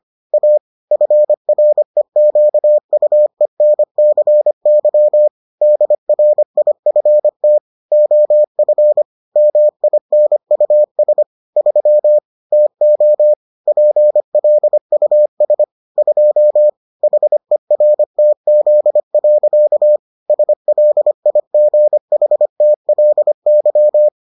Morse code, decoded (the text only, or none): A FREQUENCY DRIFT OF MINUS 3 TO PLUS 2 HERTZ. SLIGHTLY